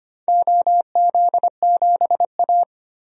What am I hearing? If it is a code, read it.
Morse code, heard as a keyed tone, about 25 wpm, 700 Hz. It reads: OZ7A
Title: DXCC-100 Call Signs - 4 Characters - 25wpm